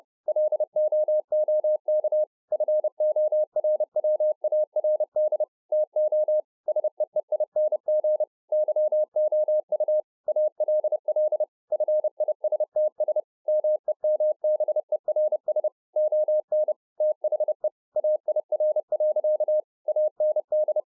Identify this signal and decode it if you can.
Morse code, heard as a keyed tone, about 30 wpm, 610 Hz. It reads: LOOK FORWARD TO SEEING YOU ALL FISTS MEMBERS ON THE AIR. AND